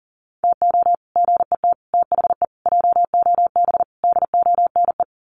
Morse code, heard as a keyed tone, about 40 wpm, 700 Hz. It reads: TO GET THE JOB DONE
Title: Sets of 5 Words Encore2 40wpm